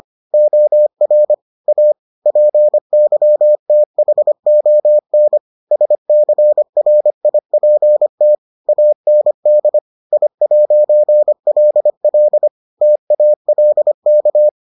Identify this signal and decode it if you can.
Morse code, heard as a keyed tone, about 25 wpm, 600 Hz. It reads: OR A PYTHON SCRIPT AND I'LL TALK